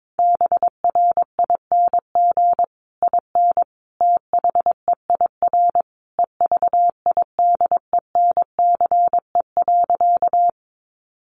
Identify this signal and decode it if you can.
Morse code, heard as a keyed tone, about 22 wpm, 700 Hz. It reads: BRING IN THEIR EVIDENCE.
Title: Sentences from Top 1300 Words 22wpm